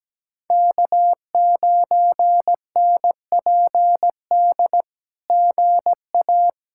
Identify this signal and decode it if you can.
Morse code, heard as a keyed tone, about 17 wpm, 700 Hz. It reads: K9NPD GA
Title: SOTA QSO Practice 17wpm